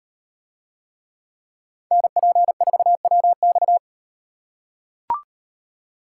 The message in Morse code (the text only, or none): NP4WX